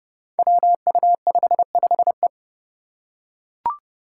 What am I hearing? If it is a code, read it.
Morse code, heard as a keyed tone, about 30 wpm, 700 Hz. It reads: WU55E
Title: Five Challenging Letter-Number Combinations 30wpm